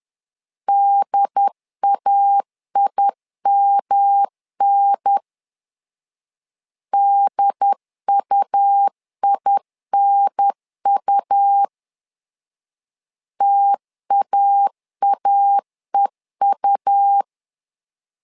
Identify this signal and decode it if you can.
Morse code, heard as a keyed tone, about 11 wpm, 800 Hz. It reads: DAIMN DUINU TAAEU
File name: LEZIONI / 3 MWRO / 3 M 10wpm 1828